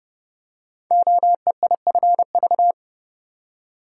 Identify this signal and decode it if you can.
Morse code, heard as a keyed tone, about 28 wpm, 700 Hz. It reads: OEIFV
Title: Code Groups - 5 Letters - 30wpm